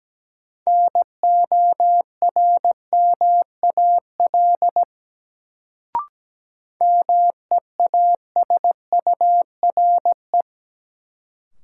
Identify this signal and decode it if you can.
Morse code, heard as a keyed tone, about 17 wpm, 700 Hz. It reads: NORMAL MEASURE
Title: Sentences from Top 1200 Words - New Word Drill Alt - 17wpm